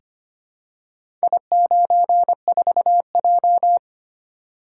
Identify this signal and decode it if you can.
Morse code, heard as a keyed tone, about 25 wpm, 700 Hz. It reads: I94J
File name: practice / 25wpm / Four Letter-Number Combinations - 25wpm